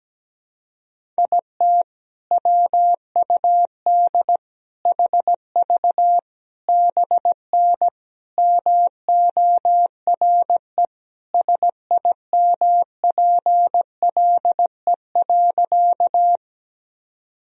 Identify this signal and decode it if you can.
Morse code, heard as a keyed tone, about 17 wpm, 700 Hz. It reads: IT WUD HV BN MORE SIMPLE.
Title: Sentences from Top 400 Words and Abbreviations 17wpm